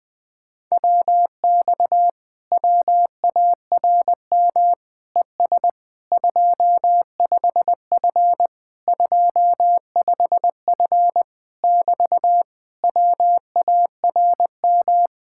Morse code, heard as a keyed tone, about 21 wpm, 700 Hz. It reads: WX WARM ES 25F 25F <BT> WARM
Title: Speed-Racing Common QSO Phrases - 20wpm